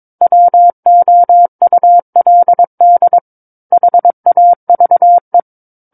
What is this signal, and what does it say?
Morse code, heard as a keyed tone, about 22 wpm, 700 Hz. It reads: WOULD HAVE
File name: 06-Chapter-06 22 WPM